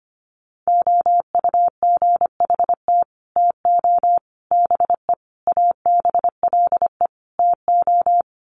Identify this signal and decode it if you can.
Morse code, heard as a keyed tone, about 25 wpm, 700 Hz. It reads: OUGHT TO BE ABLE TO
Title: Sets of 5 Words Encore2 25wpm